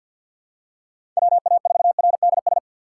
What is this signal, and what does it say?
Morse code, heard as a keyed tone, about 46 wpm, 700 Hz. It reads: WA4RDS
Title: USA Call Signs - 2x Character Spacing - 50wpm